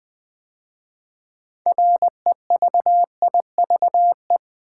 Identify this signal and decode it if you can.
Morse code, heard as a keyed tone, about 20 wpm, 700 Hz. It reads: REVIVE
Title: 6 Letter Words - First 3 Letters Given - 20wpm